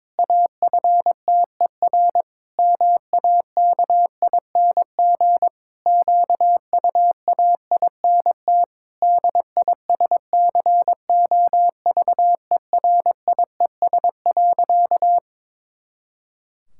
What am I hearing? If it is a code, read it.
Morse code, heard as a keyed tone, about 22 wpm, 700 Hz. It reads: AFTER MAKING QUAINT DISCOVERIES.
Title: Pangram Challenge v2 - 22wpm